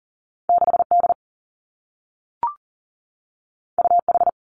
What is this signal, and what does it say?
Morse code, heard as a keyed tone, about 40 wpm, 700 Hz. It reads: 6 D U H
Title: Two Challenging Letter-Number Combinations 40wpm